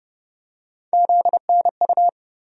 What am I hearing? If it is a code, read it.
Morse code, heard as a keyed tone, about 30 wpm, 700 Hz. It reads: ZNU
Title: USA Call Sign Postfixes - 3 Letters 30wpm